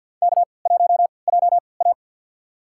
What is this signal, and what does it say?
Morse code, heard as a keyed tone, about 53 wpm, spaced as for 29 wpm, 700 Hz. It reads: K1JA
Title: USA Call Signs - 4 Characters - 3x Character Spacing - 50wpm